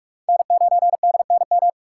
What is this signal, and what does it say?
Morse code, heard as a keyed tone, about 63 wpm, spaced as for 54 wpm, 700 Hz. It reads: N9DNM
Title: Speed Racing - USA Call Signs 45wpm